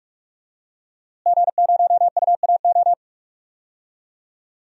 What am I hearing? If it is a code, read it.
Morse code, heard as a keyed tone, about 44 wpm, 700 Hz. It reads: G0UAO
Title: DXCC-100 Call Signs - 5 Characters - Repeated 2x - 45wpm